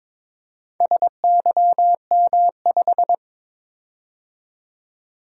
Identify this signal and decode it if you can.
Morse code, heard as a keyed tone, about 22 wpm, 700 Hz. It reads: SYM5
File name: practice / 22wpm / Four Letter-Number Combinations - 22wpm